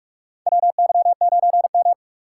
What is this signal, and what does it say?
Morse code, heard as a keyed tone, about 44 wpm, 700 Hz. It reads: WY9M
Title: USA Call Signs - 4 Characters - 3x Character Spacing - 45wpm